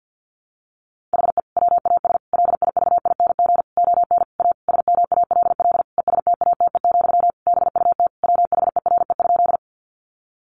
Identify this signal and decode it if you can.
Morse code, heard as a keyed tone, about 49 wpm, 700 Hz. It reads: HE WAS LIVING ON A SMALL ESTATE, BUT WHERE?
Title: Sentences from Top 1200 Words 50wpm